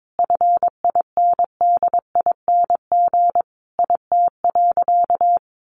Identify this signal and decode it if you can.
Morse code, heard as a keyed tone, about 22 wpm, 700 Hz. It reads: FINDING IT.
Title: Sentences from Top 1100 Words 22wpm